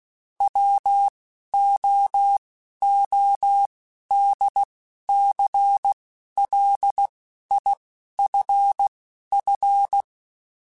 Morse code, heard as a keyed tone, about 16 wpm, spaced as for 11 wpm, 800 Hz. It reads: WOODCLIFF